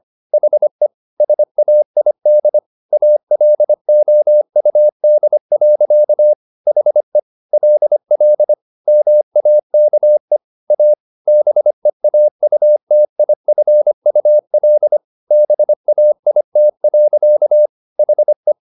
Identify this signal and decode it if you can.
Morse code, heard as a keyed tone, about 25 wpm, 600 Hz. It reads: HE SAID ALOUD. HE LL MAKE A BEAUTIFUL BAIT. HE
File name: the old man and the sea part 1 25wpm